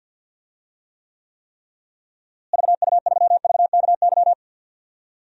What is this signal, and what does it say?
Morse code, heard as a keyed tone, about 47 wpm, 700 Hz. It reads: VU3VKY
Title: DXCC-100 Call Signs - 6 Characters - Repeated 3x - 50wpm